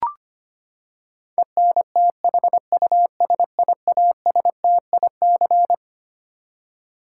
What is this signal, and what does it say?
Morse code, heard as a keyed tone, about 25 wpm, 700 Hz. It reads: ENTHUSIASTIC
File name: Twelve Letter Words - 25wpm